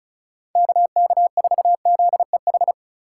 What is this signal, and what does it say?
Morse code, heard as a keyed tone, about 34 wpm, 700 Hz. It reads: KK4ZEH
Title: USA Call Signs - 6 Characters - 2x Character Spacing - 35wpm